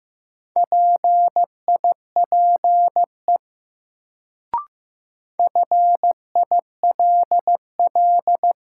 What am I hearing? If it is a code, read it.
Morse code, heard as a keyed tone, about 15 wpm, 700 Hz. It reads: PIPE FILL
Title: Sentences from Top 1500 Words - New Word Drill Alt - 15wpm